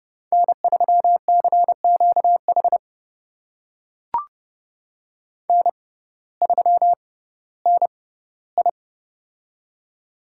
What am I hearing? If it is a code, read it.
Morse code, heard as a keyed tone, about 32 wpm, spaced as for 17 wpm, 700 Hz. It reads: N3CQH N3NI
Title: USA Call Signs - 6x Character Spacing - 30wpm